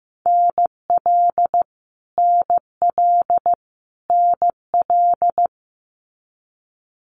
Morse code, heard as a keyed tone, about 15 wpm, 700 Hz. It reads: NL NL NL
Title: Canadian Provinces and Territories - Repeated 3x - 15wpm